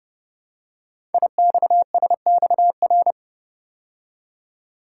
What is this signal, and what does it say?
Morse code, heard as a keyed tone, about 30 wpm, 690 Hz. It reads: IXSXR
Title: Code Groups - 5 Letter:Numbers - 30wpm